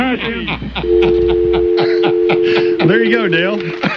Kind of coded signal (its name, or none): telephone call progress tone